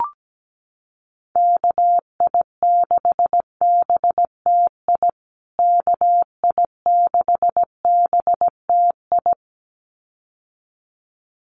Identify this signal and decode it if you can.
Morse code, heard as a keyed tone, about 17 wpm, 700 Hz. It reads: KI6BTI KI6BTI
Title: USA Call Signs Repeated 2x 17wpm